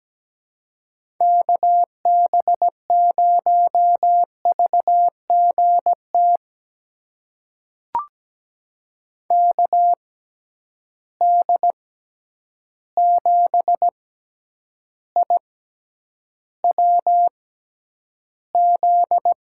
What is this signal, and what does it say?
Morse code, heard as a keyed tone, about 17 wpm, spaced as for 9 wpm, 700 Hz. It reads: KB0VGT KD7IWZ
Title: USA Call Signs - 6 Characters - 6x Character Spacing - 17wpm